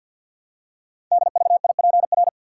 Morse code, heard as a keyed tone, about 49 wpm, 700 Hz. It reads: DVIPR
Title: Code Groups - 5 Letters - 50wpm